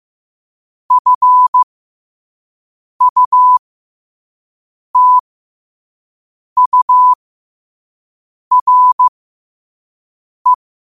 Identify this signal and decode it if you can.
Morse code, heard as a keyed tone, about 15 wpm, spaced as for 5 wpm, 1000 Hz. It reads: FUTURE
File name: qotdfiles / 5wpm